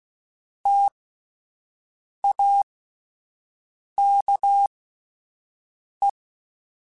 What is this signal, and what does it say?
Morse code, heard as a keyed tone, about 16 wpm, spaced as for 5 wpm, 800 Hz. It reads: TAKE